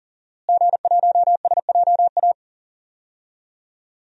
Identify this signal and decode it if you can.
Morse code, heard as a keyed tone, about 40 wpm, 700 Hz. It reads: G1SJA
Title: DXCC-100 Call Signs - 5 Characters - Repeated 2x - 40wpm